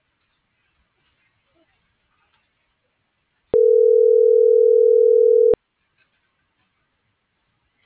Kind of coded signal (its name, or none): telephone call progress tone